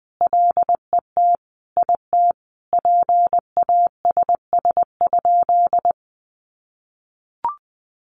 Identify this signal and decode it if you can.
Morse code, heard as a keyed tone, about 19 wpm, 700 Hz. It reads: LET IT PASS?
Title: Sentences from Top 400 Words 20wpm